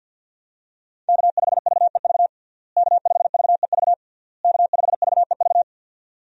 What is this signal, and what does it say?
Morse code, heard as a keyed tone, about 53 wpm, 700 Hz. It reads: K5VEV K5VEV K5VEV
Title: USA Call Signs - 5 Characters - Repeated 3x - 50wpm